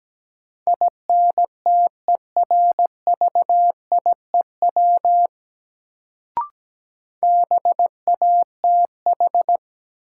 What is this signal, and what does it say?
Morse code, heard as a keyed tone, about 17 wpm, 700 Hz. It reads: INTERVIEW BATH
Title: Sentences from Top 1600 Words - New Word Drill Alt - 17wpm